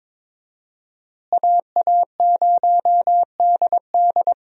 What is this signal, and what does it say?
Morse code, heard as a keyed tone, about 21 wpm, 700 Hz. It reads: AA0DD
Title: USA Call Signs - 5 Characters - Repeated 3x - 22wpm